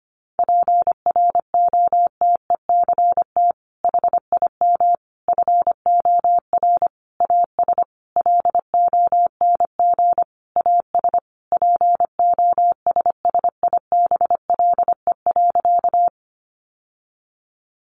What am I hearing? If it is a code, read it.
Morse code, heard as a keyed tone, about 25 wpm, 700 Hz. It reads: PROTECT HIM FOR AS LONG AS POSSIBLE.
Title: Sentences from Top 900 Words 25wpm